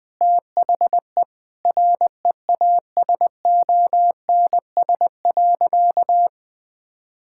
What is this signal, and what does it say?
Morse code, heard as a keyed tone, about 20 wpm, 700 Hz. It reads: THE REASONS.
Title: Sentences with 4-Word Phrases - Spoken Beginning - 20wpm